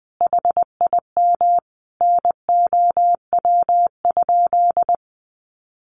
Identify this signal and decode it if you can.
Morse code, heard as a keyed tone, about 20 wpm, 700 Hz. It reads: HIM NOW?
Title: Sentences from Top 1400 Words 20wpm